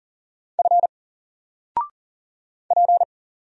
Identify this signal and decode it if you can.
Morse code, heard as a keyed tone, about 39 wpm, 700 Hz. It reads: F P
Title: Single Letter-Number Alt 40wpm